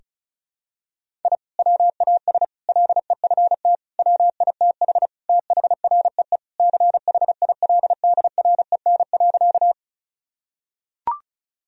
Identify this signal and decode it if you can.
Morse code, heard as a keyed tone, about 36 wpm, 700 Hz. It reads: I WAS LEFT WITH THREE CHILDREN.
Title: Sentences from Top 1000 Words - Review Part 1 - 35wpm